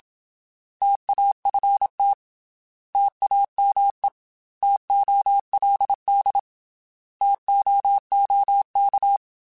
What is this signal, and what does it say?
Morse code, heard as a keyed tone, about 27 wpm, 800 Hz. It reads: TAFT TAME TOLD TOOK
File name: T Prefixes / T mp3s